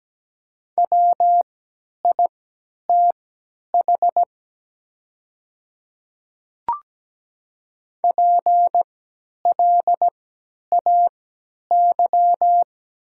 Morse code, heard as a keyed tone, about 17 wpm, spaced as for 10 wpm, 700 Hz. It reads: WITH PLAY
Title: 3x Character Spacing - Top 300 Words - 17wpm